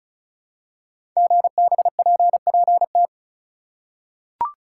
Morse code, heard as a keyed tone, about 36 wpm, 690 Hz. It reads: GBPPT
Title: Code Groups - 5 Letters - 35wpm